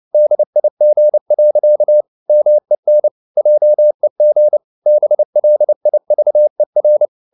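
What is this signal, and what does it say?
Morse code, heard as a keyed tone, about 29 wpm, 600 Hz. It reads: DIG. MEN JEG BLIVER
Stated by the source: Fast (QRQ) Morse at 29 wpm, from CW med Gnister.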